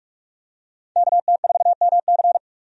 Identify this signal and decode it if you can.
Morse code, heard as a keyed tone, about 44 wpm, 700 Hz. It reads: KT4MC
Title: USA Call Signs - 5 Characters - 45wpm